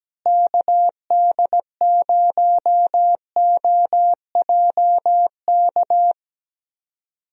Morse code, heard as a keyed tone, about 17 wpm, 700 Hz. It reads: KD0OJK
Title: USA Call Signs - 6 Characters - Repeated 3x - 17wpm